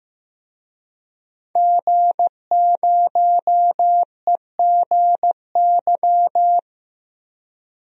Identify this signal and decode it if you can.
Morse code, heard as a keyed tone, about 15 wpm, 700 Hz. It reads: G0EGY
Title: DXCC-100 Call Signs - 5 Characters - Repeated 3x - 15wpm